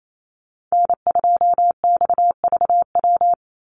Morse code, heard as a keyed tone, about 28 wpm, 700 Hz. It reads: N2XVW
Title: USA Call Signs 28wpm